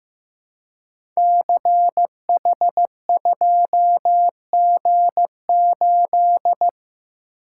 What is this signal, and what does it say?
Morse code, heard as a keyed tone, about 15 wpm, 700 Hz. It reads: CH2G8
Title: Code Groups - 5 Letter:Numbers - 15wpm